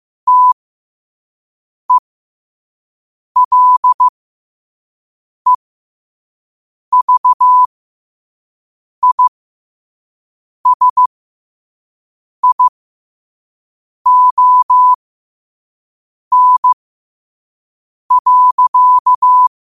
Morse code, heard as a keyed tone, about 15 wpm, spaced as for 5 wpm, 1000 Hz. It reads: TELEVISION.
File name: qotdfiles / 5wpm